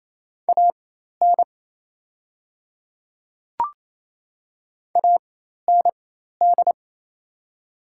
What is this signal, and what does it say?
Morse code, heard as a keyed tone, about 28 wpm, spaced as for 13 wpm, 690 Hz. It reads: AN AND
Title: 4x Character Spacing - Top 100 Words - 28wpm